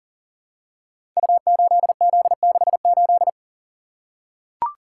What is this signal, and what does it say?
Morse code, heard as a keyed tone, about 42 wpm, 700 Hz. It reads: U8Z68
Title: Code Groups - 5 Letter:Numbers - 40wpm